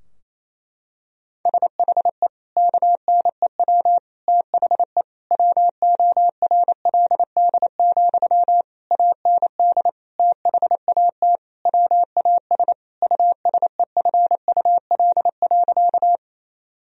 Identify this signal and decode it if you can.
Morse code, heard as a keyed tone, about 28 wpm, 700 Hz. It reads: SHE KNEW THE WORLD, AND THAT WAS USEFUL.
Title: Sentences from Top 1000 Words - Review Part 4 - 28wpm